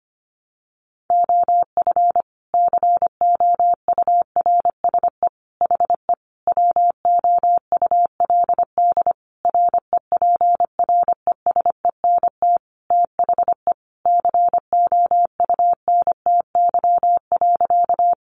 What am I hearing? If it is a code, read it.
Morse code, heard as a keyed tone, about 25 wpm, 700 Hz. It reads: OF COURSE HE WOULD REPRESENT THE COUNTY.
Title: Sentences from Top 1200 Words 25wpm